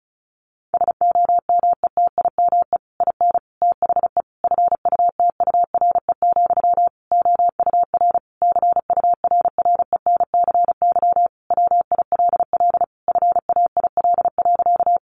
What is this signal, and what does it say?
Morse code, heard as a keyed tone, about 34 wpm, 700 Hz. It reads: SOMETIME IN THE FUTURE, OUR CURRENCY WILL FAIL.
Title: Sentences from Top 1500 Words 35wpm